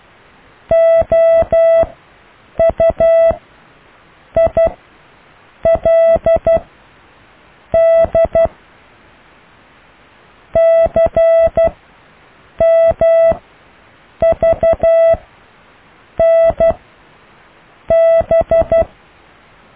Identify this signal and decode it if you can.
Morse code, heard as a keyed tone, about 12 wpm, spaced as for 6 wpm, 660 Hz. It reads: OUILD CMVNB
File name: cw / IJLUM+CDRO+EKTPV+BFHNW